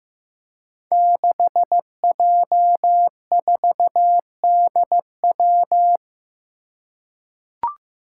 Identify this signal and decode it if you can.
Morse code, heard as a keyed tone, about 15 wpm, 700 Hz. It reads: 6J4DW
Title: Five Challenging Letter-Number Combinations 15wpm